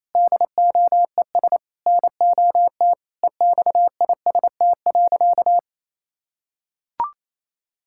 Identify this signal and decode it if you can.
Morse code, heard as a keyed tone, about 28 wpm, 700 Hz. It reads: DOES NOT EXIST.
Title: Sentences from Top 1000 Words - Review Part 5 - 28wpm